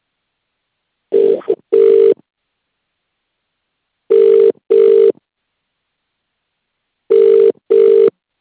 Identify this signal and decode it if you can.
Ringback tone (UK), 400 + 450 Hz.